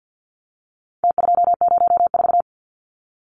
Morse code, heard as a keyed tone, about 48 wpm, 700 Hz. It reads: T204
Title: Numbers - Contest Style - 4-Digit 50wpm